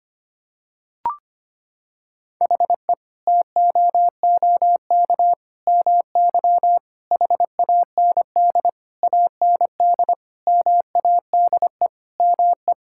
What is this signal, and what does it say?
Morse code, heard as a keyed tone, about 25 wpm, 700 Hz. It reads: HE TOOK MY HAND AND MADE ME
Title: Sentences from Top 1000 Words - Review Part 6 - 25wpm